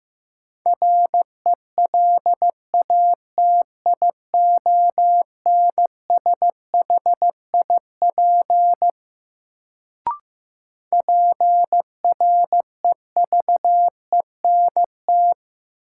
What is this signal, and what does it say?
Morse code, heard as a keyed tone, about 15 wpm, 700 Hz. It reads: RELATIONSHIP PREVENT
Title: Sentences from Top 1100 Words - New Word Drill - 15wpm